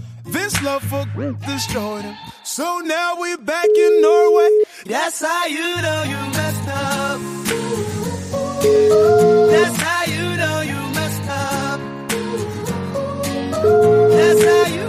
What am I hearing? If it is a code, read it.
Ringback tone (Europe, 425 Hz), with speech.